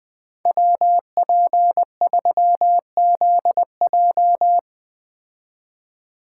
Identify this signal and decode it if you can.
Morse code, heard as a keyed tone, about 20 wpm, 700 Hz. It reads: WP3ZJ